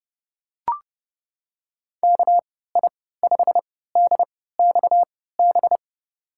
Morse code, heard as a keyed tone, about 30 wpm, spaced as for 17 wpm, 700 Hz. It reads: KI5DXB
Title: USA Call Signs - 3x Character Spacing - 30wpm